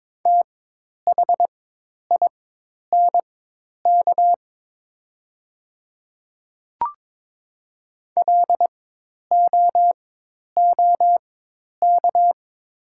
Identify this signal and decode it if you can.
Morse code, heard as a keyed tone, about 23 wpm, spaced as for 10 wpm, 690 Hz. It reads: THINK LOOK
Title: 4x Character Spacing - Top 300 Words - 22wpm